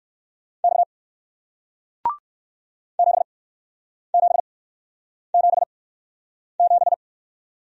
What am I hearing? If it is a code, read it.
Morse code, heard as a keyed tone, about 61 wpm, spaced as for 9 wpm, 700 Hz. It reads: X 7777